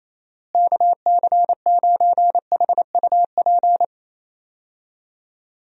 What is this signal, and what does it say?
Morse code, heard as a keyed tone, about 28 wpm, 700 Hz. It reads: KC9HUP
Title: USA Call Signs - 6 Characters - Repeated 3x - 28wpm